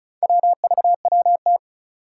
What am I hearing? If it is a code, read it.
Morse code, heard as a keyed tone, about 36 wpm, 700 Hz. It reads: WVWT
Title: Four Challenging Letter-Number Combinations 35wpm